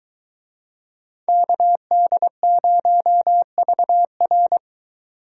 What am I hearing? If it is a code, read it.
Morse code, heard as a keyed tone, about 24 wpm, 700 Hz. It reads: KD0VR